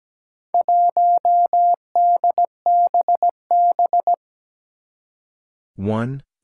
Morse code, heard as a keyed tone, about 17 wpm, 700 Hz. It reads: 1DBB
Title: Four Challenging Letter-Number Combinations 17wpm